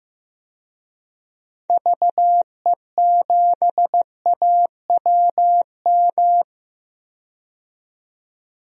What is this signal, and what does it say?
Morse code, heard as a keyed tone, about 15 wpm, 700 Hz. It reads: VE7AWM
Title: DXCC-100 Call Signs - 6 Characters - 15wpm